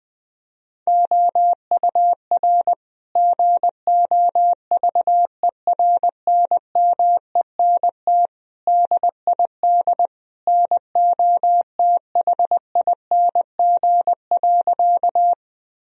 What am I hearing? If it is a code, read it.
Morse code, heard as a keyed tone, about 20 wpm, 700 Hz. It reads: OUR GOVERNMENT DID NOTHING.
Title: Sentences from Top 1000 Words - Review Part 3 - 20wpm